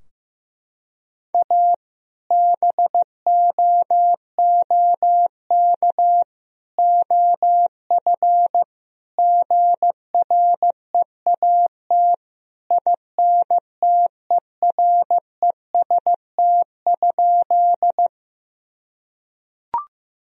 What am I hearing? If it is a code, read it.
Morse code, heard as a keyed tone, about 16 wpm, 700 Hz. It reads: A BOOK OF GREAT INTEREST?
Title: Sentences from Top 1000 Words - Review Part 2 - 15wpm